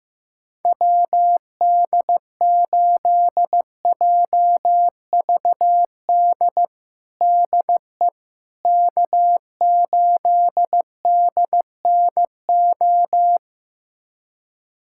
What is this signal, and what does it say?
Morse code, heard as a keyed tone, about 15 wpm, 700 Hz. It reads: WD8JVD DE K8DNO
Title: USA Callsign DE Callsign 15wpm